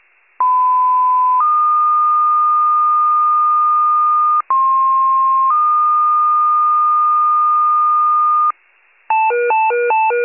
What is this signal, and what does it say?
Two-tone page, heard as a one-second tone, then a three-second tone, about 1010 then 1220 Hz, about 1060 then 1220 Hz. It is heard from 0.4 to 8.5 s.
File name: pager / 94 RESCUE OFC / wcalert